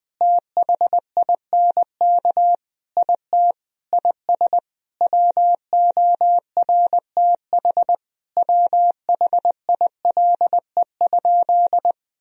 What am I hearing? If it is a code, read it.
Morse code, heard as a keyed tone, about 20 wpm, 700 Hz. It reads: THINK IT IS WORTH WHILE?
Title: Sentences from Top 1000 Words - Review Part 1 - 20wpm